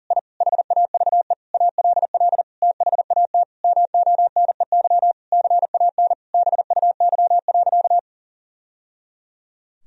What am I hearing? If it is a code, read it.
Morse code, heard as a keyed tone, about 40 wpm, 700 Hz. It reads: I HAVE ALL THAT MONEY CAN BUY.
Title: Sentences from Top 1000 Words - Review Part 4 - 40wpm